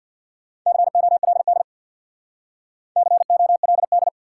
Morse code, heard as a keyed tone, about 69 wpm, 700 Hz. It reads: COLD COLD